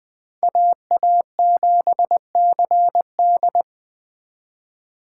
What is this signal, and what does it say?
Morse code, heard as a keyed tone, about 21 wpm, 700 Hz. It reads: AA7CD